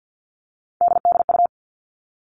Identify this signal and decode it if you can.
Morse code, heard as a keyed tone, about 50 wpm, 700 Hz. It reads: DDU
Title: All Cut Numbers - 3-Digit 50wpm